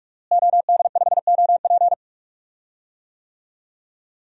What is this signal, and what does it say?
Morse code, heard as a keyed tone, about 41 wpm, 700 Hz. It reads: OD5OP
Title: DXCC-100 Call Signs - 5 Characters - Repeated 2x - 45wpm